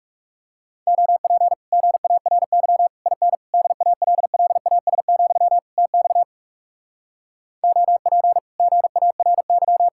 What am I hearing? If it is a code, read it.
Morse code, heard as a keyed tone, about 40 wpm, 700 Hz. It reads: OP GARY IN DALLAS, TX OP GARY